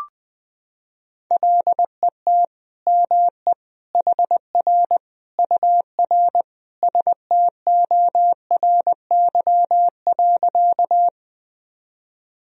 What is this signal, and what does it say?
Morse code, heard as a keyed tone, about 20 wpm, 700 Hz. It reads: LET ME HR UR STORY.
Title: Sentences from Top 300 Words and Abbreviations 20wpm